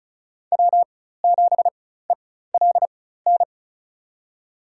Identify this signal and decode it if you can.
Morse code, heard as a keyed tone, about 36 wpm, spaced as for 16 wpm, 700 Hz. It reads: W7ELN
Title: USA Call Signs - 5 Characters - 4x Character Spacing - 35wpm